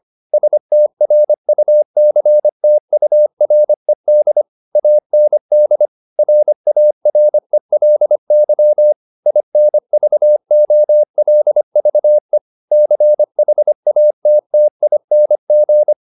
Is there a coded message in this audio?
Yes: Morse code from 0.3 to 15.9 s, heard as a keyed tone, about 25 wpm, 600 Hz. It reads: STRUCTURED AND RARELY INVOLVE CHATTING